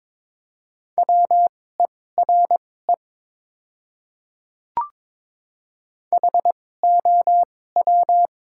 Morse code, heard as a keyed tone, about 22 wpm, spaced as for 16 wpm, 700 Hz. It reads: WERE HOW
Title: 2x Character Spacing - Top 100 Words - 22wpm